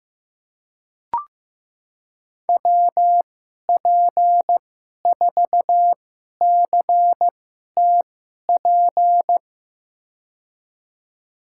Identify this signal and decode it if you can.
Morse code, heard as a keyed tone, about 15 wpm, spaced as for 11 wpm, 700 Hz. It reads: WP4CTP
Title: USA Call Signs - 6 Characters - 2x Character Spacing - 15wpm